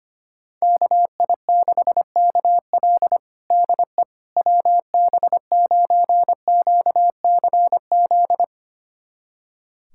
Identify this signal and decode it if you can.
Morse code, heard as a keyed tone, about 25 wpm, 700 Hz. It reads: KI6KL DE WB9QCZ
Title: USA Callsign DE Callsign 25wpm